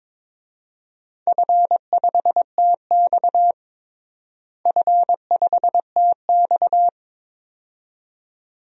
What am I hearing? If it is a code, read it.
Morse code, heard as a keyed tone, about 22 wpm, 700 Hz. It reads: F5TX F5TX
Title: DXCC-100 Call Signs - 4 Characters - Repeated 2x - 22wpm